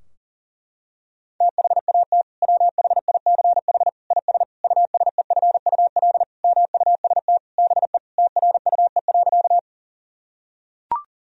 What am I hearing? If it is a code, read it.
Morse code, heard as a keyed tone, about 40 wpm, 700 Hz. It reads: THAT WHICH IS USEFUL MUST BE TRUE.
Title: Sentences from Top 1000 Words - Review Part 4 - 40wpm